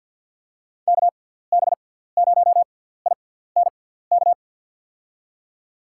Morse code, heard as a keyed tone, about 51 wpm, spaced as for 17 wpm, 700 Hz. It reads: KB0INK
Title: USA Call Signs - 6 Characters - 6x Character Spacing - 50wpm